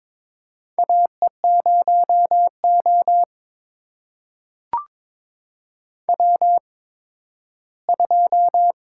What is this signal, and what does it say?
Morse code, heard as a keyed tone, about 22 wpm, spaced as for 13 wpm, 700 Hz. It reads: AE0O W2
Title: USA Call Signs - 4 Characters - 8x Character Spacing - 22wpm